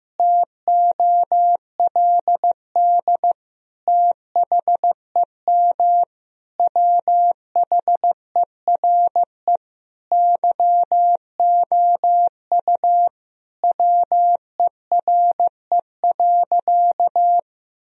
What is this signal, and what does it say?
Morse code, heard as a keyed tone, about 15 wpm, 700 Hz. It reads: TOLD THEM WHERE YOU WERE.
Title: Sentences from Top 1000 Words - Review Part 4 - 15wpm